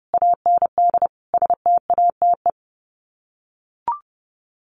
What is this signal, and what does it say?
Morse code, heard as a keyed tone, about 30 wpm, 700 Hz. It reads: AND STATE
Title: Sets of 5 Words 30wpm